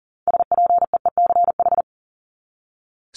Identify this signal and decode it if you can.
Morse code, heard as a keyed tone, about 40 wpm, 700 Hz. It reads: SPEECH
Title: Top 1751-2000 Words 40wpm